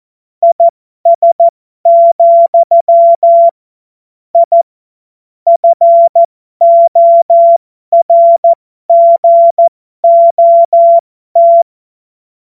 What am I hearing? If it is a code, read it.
Morse code, heard as a keyed tone, about 14 wpm, spaced as for 12 wpm, 680 Hz. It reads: IS, I FORGOT
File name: Morsekurs / Witze